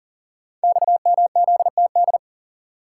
Morse code, heard as a keyed tone, about 41 wpm, 700 Hz. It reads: XMZTD
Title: Code Groups - 5 Letter:Numbers - 40wpm